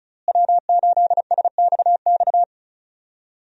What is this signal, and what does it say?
Morse code, heard as a keyed tone, about 36 wpm, 700 Hz. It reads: W8SXX
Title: USA Call Signs - 5 Characters - Repeated 3x - 35wpm